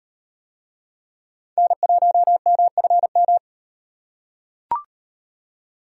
Morse code, heard as a keyed tone, about 35 wpm, 700 Hz. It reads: N1MFM